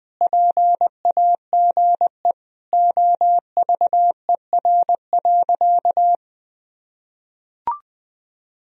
Morse code, heard as a keyed tone, about 20 wpm, 700 Hz. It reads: PAGE OVER.
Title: Sentences from Top 1000 Words - Review Part 1 - 20wpm